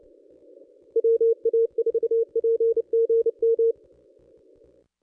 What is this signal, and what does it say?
Morse code, heard as a keyed tone, about 30 wpm, 460 Hz. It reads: WA4PGM
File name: contests / ss-cw-k5tr-op-n5rz -1200